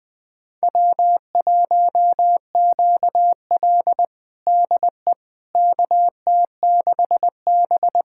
Morse code, heard as a keyed tone, about 21 wpm, 700 Hz. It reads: W1QL DE KT6B
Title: USA Callsign DE Callsign - 4 Characters - 20wpm